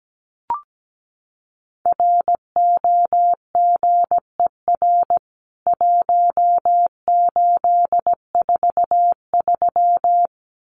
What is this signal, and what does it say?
Morse code, heard as a keyed tone, about 17 wpm, 700 Hz. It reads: ROGER 1843
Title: CWT Exchange 17wpm